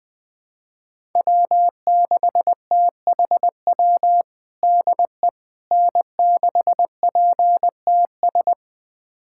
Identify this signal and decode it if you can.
Morse code, heard as a keyed tone, about 20 wpm, 700 Hz. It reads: W6THW DE N6PTS
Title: USA Callsign DE Callsign - 5 Characters - 20wpm